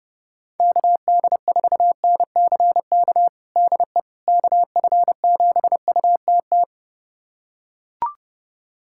Morse code, heard as a keyed tone, about 29 wpm, 700 Hz. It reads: KD4NCK DE KF7UTT
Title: USA Callsign DE Callsign 30wpm